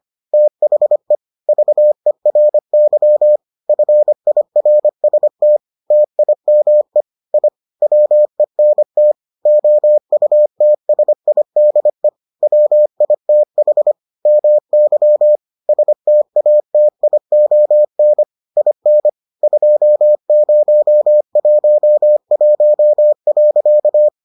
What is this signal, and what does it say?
Morse code, heard as a keyed tone, about 25 wpm, 600 Hz. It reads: THE VERY FIRST TIME I WENT OUTSIDE WITH MY STATION IN 2011.